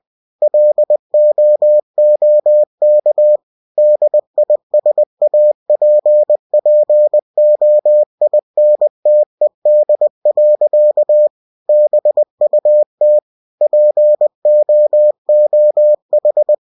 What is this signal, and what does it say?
Morse code, heard as a keyed tone, about 20 wpm, 600 Hz. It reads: LOOK DISAPPOINTED. BUT POOH
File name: cw-books / winnie-the-pooh / winnie-20wpm-page-0063